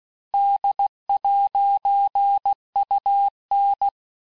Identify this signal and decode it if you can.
Morse code, heard as a keyed tone, about 16 wpm, 800 Hz. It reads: D'UN
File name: texte-clair-vitesse-15mpm-puis-20mpm